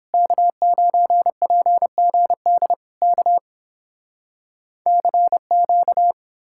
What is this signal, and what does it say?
Morse code, heard as a keyed tone, about 29 wpm, 700 Hz. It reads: K9PGD K CQ